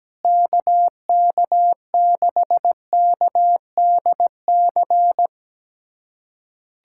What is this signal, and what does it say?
Morse code, heard as a keyed tone, about 17 wpm, 700 Hz. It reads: KK6KDC
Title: USA Callsign DE Callsign - 6 Characters - 17wpm